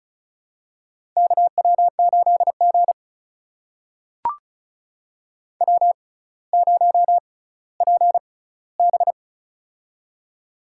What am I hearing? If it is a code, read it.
Morse code, heard as a keyed tone, about 36 wpm, spaced as for 18 wpm, 700 Hz. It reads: KW8G W0PB
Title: USA Call Signs - 4 Characters - 6x Character Spacing - 35wpm